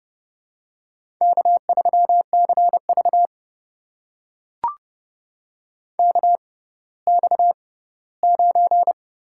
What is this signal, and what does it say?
Morse code, heard as a keyed tone, about 30 wpm, spaced as for 17 wpm, 700 Hz. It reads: K3CV KX9
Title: USA Call Signs - 4 Characters - 6x Character Spacing - 30wpm